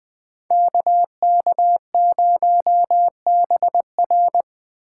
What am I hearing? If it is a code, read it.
Morse code, heard as a keyed tone, about 19 wpm, 700 Hz. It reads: KK0BR